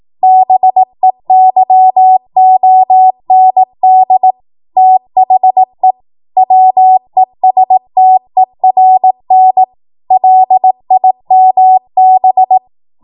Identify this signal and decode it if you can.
Morse code, heard as a keyed tone, about 18 wpm, 750 Hz. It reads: BEYOND THE WESTERN LIMB
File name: Morse / 18WPM